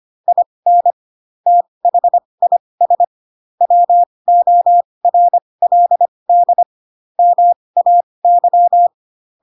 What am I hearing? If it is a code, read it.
Morse code, heard as a keyed tone, about 25 wpm, spaced as for 20 wpm, 700 Hz. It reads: IN THIS WORLD MAY
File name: Pinocchio / Pinocchio ch7 15 25 ik1hge